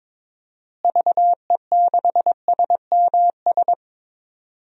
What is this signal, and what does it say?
Morse code, heard as a keyed tone, about 22 wpm, 700 Hz. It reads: VE6SMS
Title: DXCC-100 Call Signs - 6 Characters - Repeated 2x - 22wpm